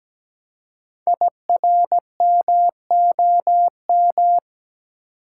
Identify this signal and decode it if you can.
Morse code, heard as a keyed tone, about 17 wpm, 700 Hz. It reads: IRMOM
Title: Code Groups - 5 Letters - 17wpm